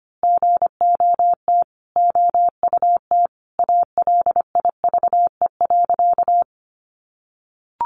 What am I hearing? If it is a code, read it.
Morse code, heard as a keyed tone, about 25 wpm, 700 Hz. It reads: GOT OUT ALIVE.
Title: A Sound Of Thunder 25wpm